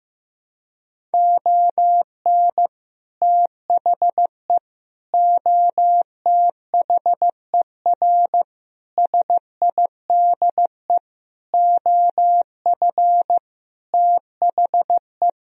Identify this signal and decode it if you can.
Morse code, heard as a keyed tone, about 16 wpm, 700 Hz. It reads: ON THE OTHER SIDE OF THE
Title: Sentences from Top 1000 Words - Review Part 1 - 15wpm